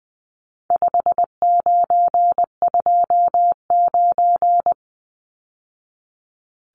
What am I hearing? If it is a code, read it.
Morse code, heard as a keyed tone, about 19 wpm, 700 Hz. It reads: 5929
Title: Numbers - 4-Digit 20wpm